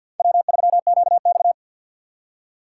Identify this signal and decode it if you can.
Morse code, heard as a keyed tone, about 53 wpm, 700 Hz. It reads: W3QX
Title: USA Call Signs - 4 Characters - 6x Character Spacing - 50wpm